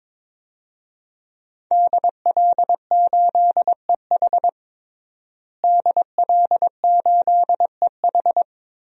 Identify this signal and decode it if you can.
Morse code, heard as a keyed tone, about 21 wpm, 700 Hz. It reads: DL8EH DL8EH
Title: DXCC-100 Call Signs - 5 Characters - Repeated 2x - 22wpm